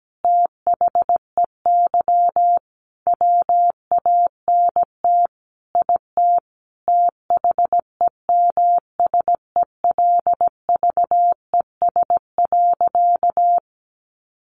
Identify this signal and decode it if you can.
Morse code, heard as a keyed tone, about 17 wpm, 700 Hz. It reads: THEY WANT IT THEMSELVES.
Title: Sentences from Top 800 Words 17wpm